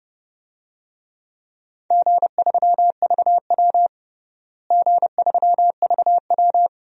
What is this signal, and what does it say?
Morse code, heard as a keyed tone, about 30 wpm, 700 Hz. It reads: G3VW G3VW
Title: DXCC-100 Call Signs - 4 Characters - Repeated 2x - 30wpm